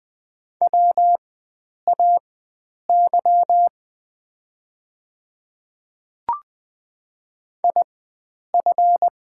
Morse code, heard as a keyed tone, about 20 wpm, spaced as for 9 wpm, 690 Hz. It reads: WAY IF
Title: 4x Character Spacing - Top 100 Words - 20wpm